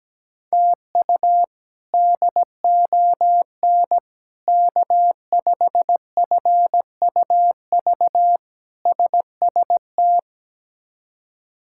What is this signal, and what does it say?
Morse code, heard as a keyed tone, about 17 wpm, 700 Hz. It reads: TU DON K5FUV SST
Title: K1USN SST QSO Practice - 17wpm